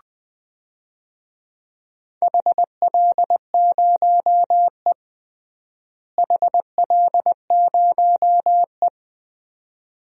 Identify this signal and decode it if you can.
Morse code, heard as a keyed tone, about 20 wpm, 700 Hz. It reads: HL0E HL0E
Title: DXCC-100 Call Signs - 4 Characters - Repeated 2x - 20wpm